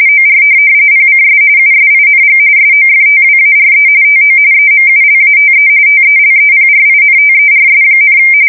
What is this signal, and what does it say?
RTTY, about 45.45 baud. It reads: OR PCS USING SOFTWARE TO EMULATE TELEPRINTERS,
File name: fsk170 / teletype-RTTY